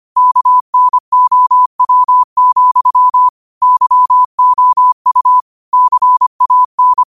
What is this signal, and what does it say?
Morse code, heard as a keyed tone, about 25 wpm, 1000 Hz. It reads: KNOW, YOU CAN
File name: qotdfiles / 25wpm